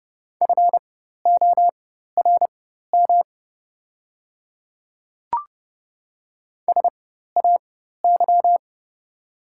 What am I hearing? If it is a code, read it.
Morse code, heard as a keyed tone, about 30 wpm, spaced as for 14 wpm, 700 Hz. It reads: FORM SAY
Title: 4x Character Spacing - Top 200 Words - 30wpm